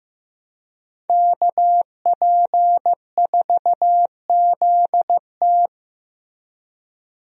Morse code, heard as a keyed tone, about 15 wpm, 700 Hz. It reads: KP4ZT
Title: USA Call Signs - 5 Characters - Repeated 2x - 15wpm